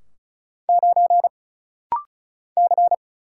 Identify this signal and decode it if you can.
Morse code, heard as a keyed tone, about 35 wpm, 700 Hz. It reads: 9 C
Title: Single Letter-Number Alt - Rapid-Fire 35wpm